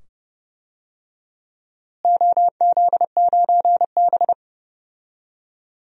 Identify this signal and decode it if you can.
Morse code, heard as a keyed tone, about 30 wpm, 700 Hz. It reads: OZ9B
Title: DXCC-100 Call Signs - 4 Characters - Repeated 2x - 30wpm